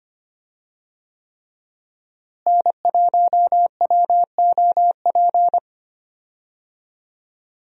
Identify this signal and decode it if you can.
Morse code, heard as a keyed tone, about 25 wpm, 700 Hz. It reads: N1WOP
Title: USA Call Signs - 5 Characters - Repeated 3x - 25wpm